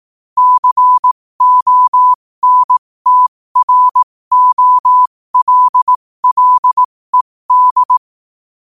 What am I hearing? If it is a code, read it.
Morse code, heard as a keyed tone, about 18 wpm, spaced as for 15 wpm, 1000 Hz. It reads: CONTROLLED
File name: qotdfiles / 15wpm